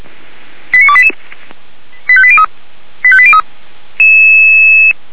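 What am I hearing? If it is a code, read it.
Five-tone selective call (ZVEI), heard as quick runs of five tones, tones of about 70 ms. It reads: 89288 85803 85803
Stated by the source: ZVEI five-tone selective call received with an SDR.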